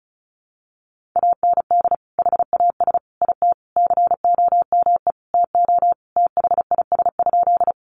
Morse code, heard as a keyed tone, about 34 wpm, 700 Hz. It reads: AND HAS IT COME TO THIS?
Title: Sentences from Top 100 Words 35wpm